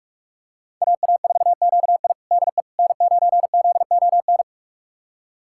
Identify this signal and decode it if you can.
Morse code, heard as a keyed tone, about 43 wpm, 700 Hz. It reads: AA4QI DE N9ZON
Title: USA Callsign DE Callsign - 5 Characters - 45wpm